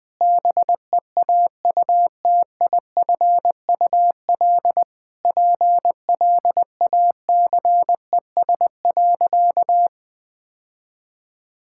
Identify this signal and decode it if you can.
Morse code, heard as a keyed tone, about 20 wpm, 700 Hz. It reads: BEAUTIFUL PLACES.
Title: Sentences with 4-Word Phrases - Spoken Beginning - 20wpm